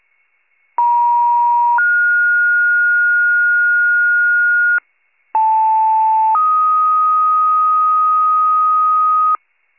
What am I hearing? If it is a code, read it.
Two-tone page, heard as a one-second tone, then a three-second tone, about 950 then 1430 Hz, about 880 then 1250 Hz, with noise. It is heard from 0.8 to 9.4 s.